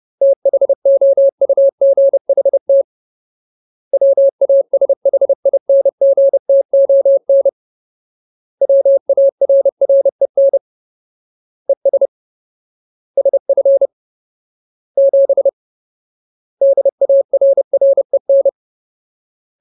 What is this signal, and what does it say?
Morse code, heard as a keyed tone, about 30 wpm, 550 Hz. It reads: THOUGHT WASHINGTON WARREN ES SF 7 DARREN
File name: Ultimate Word List / 30WPM Ultimate